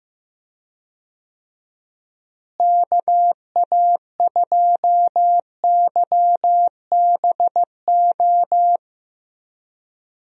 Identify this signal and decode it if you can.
Morse code, heard as a keyed tone, about 15 wpm, 700 Hz. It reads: KA2YBO
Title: USA Call Signs - 6 Characters - Repeated 2x - 15wpm